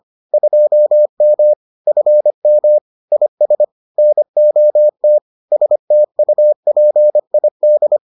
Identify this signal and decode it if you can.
Morse code, heard as a keyed tone, about 25 wpm, 600 Hz. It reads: 2M FM IS NOT STUPID